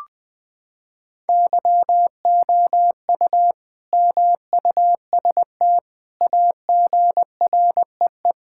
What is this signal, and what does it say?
Morse code, heard as a keyed tone, about 21 wpm, 700 Hz. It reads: YOU MUST AGREE
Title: Sentences with 9-Word Phrases - 20wpm